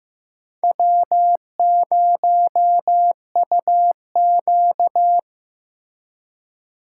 Morse code, heard as a keyed tone, about 18 wpm, 700 Hz. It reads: W0UQ
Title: Speed Racing - USA 1x2 Call Signs 15wpm